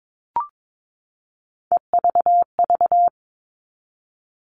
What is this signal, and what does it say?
Morse code, heard as a keyed tone, about 22 wpm, 700 Hz. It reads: EVV
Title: All Cut Numbers - 3-Digit 22wpm